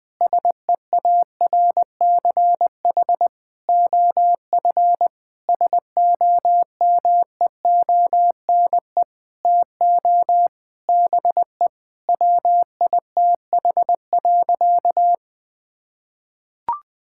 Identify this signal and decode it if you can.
Morse code, heard as a keyed tone, about 20 wpm, 700 Hz. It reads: SEARCH OF SOMEONE TO BE WITH.
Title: Sentences from Top 1000 Words - Review Part 5 - 20wpm